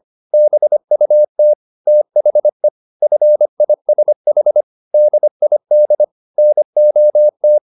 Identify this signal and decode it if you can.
Morse code, heard as a keyed tone, about 25 wpm, 600 Hz. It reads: BUT THE FISH DID NOT